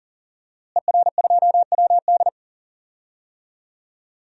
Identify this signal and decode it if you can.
Morse code, heard as a keyed tone, about 37 wpm, 700 Hz. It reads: ER2WD
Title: DXCC-100 Call Signs - 5 Characters - Repeated 3x - 40wpm